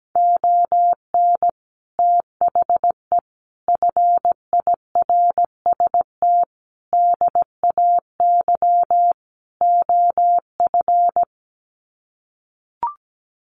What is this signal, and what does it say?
Morse code, heard as a keyed tone, about 17 wpm, 700 Hz. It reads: ON THE FIRST DAY OF
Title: Sets of 5 Words Encore 17wpm